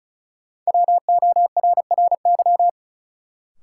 Morse code, heard as a keyed tone, about 34 wpm, 700 Hz. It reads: WORRY
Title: Sentences from Top 1100 Words - New Word Drill - 35wpm